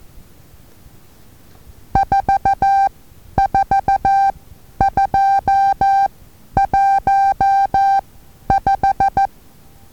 Morse code, heard as a keyed tone, about 14 wpm, spaced as for 10 wpm, 790 Hz. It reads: 44215
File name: Taviro HA6VA A